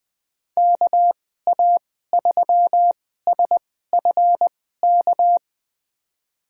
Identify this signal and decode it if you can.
Morse code, heard as a keyed tone, about 20 wpm, spaced as for 14 wpm, 700 Hz. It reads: KA3SFK
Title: USA Call Signs - 6 Characters - 2x Character Spacing - 20wpm